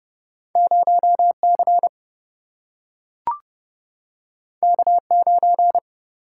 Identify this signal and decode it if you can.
Morse code, heard as a keyed tone, about 30 wpm, 700 Hz. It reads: 0C K9
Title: Single-Letter Single-Number 30wpm